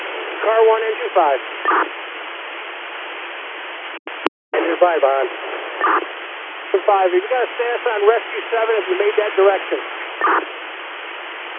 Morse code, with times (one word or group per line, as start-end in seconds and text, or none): none